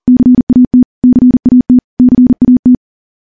Morse code, heard as a keyed tone, about 40 wpm, 250 Hz. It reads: CAT CAT CAT
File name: morse / morse-training-40WPM